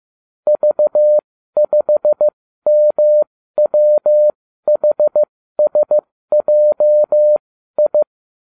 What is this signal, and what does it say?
Morse code, heard as a keyed tone, about 15 wpm, spaced as for 12 wpm, 600 Hz. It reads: V5MWHSJI